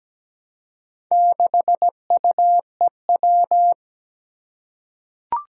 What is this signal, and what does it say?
Morse code, heard as a keyed tone, about 17 wpm, 700 Hz. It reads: 6UEW
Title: Four Challenging Letter-Number Combinations 17wpm